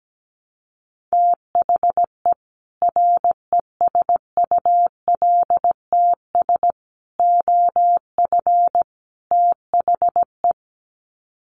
Morse code, heard as a keyed tone, about 17 wpm, 700 Hz. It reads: THE RESULTS OF THE
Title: Sets of 4 Words 17wpm